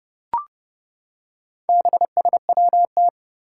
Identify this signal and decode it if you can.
Morse code, heard as a keyed tone, about 30 wpm, 700 Hz. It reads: BSWT
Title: Four Challenging Letter-Number Combinations 30wpm